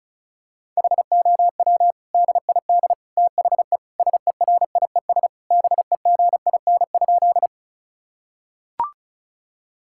Morse code, heard as a keyed tone, about 36 wpm, 700 Hz. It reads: HOW DID THE SERIES BEGIN?
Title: Sentences from Top 1000 Words - Review Part 5 - 35wpm